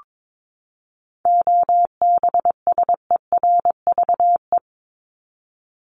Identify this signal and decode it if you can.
Morse code, heard as a keyed tone, about 22 wpm, 700 Hz. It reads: OBSERVE
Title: ED Suffix - Top 100 Words with Base Words 22wpm